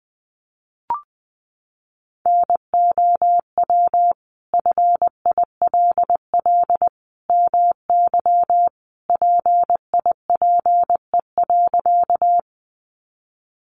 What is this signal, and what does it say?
Morse code, heard as a keyed tone, about 20 wpm, 700 Hz. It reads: NOW FILL MY PIPE.
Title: Sentences from Top 1500 Words 20wpm